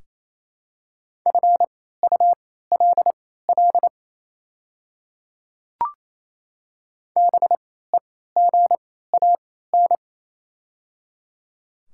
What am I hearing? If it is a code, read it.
Morse code, heard as a keyed tone, about 28 wpm, spaced as for 16 wpm, 700 Hz. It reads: FULL BEGAN
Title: 3x Character Spacing - Top 500 Words - 28wpm